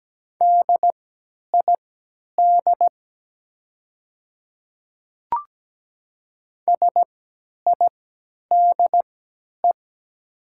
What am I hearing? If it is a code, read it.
Morse code, heard as a keyed tone, about 17 wpm, spaced as for 10 wpm, 700 Hz. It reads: DID SIDE
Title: 3x Character Spacing - Top 100 Words - 17wpm